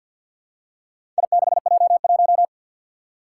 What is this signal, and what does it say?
Morse code, heard as a keyed tone, about 48 wpm, 700 Hz. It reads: I6J1
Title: Four Challenging Letter-Number Combinations 50wpm